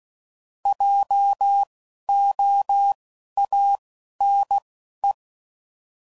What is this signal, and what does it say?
Morse code, heard as a keyed tone, about 16 wpm, spaced as for 11 wpm, 800 Hz. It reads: JOANE